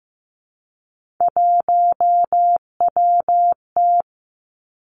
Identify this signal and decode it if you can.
Morse code, heard as a keyed tone, about 15 wpm, 700 Hz. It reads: 1WT
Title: Three Challenging Letter-Number Combinations 15wpm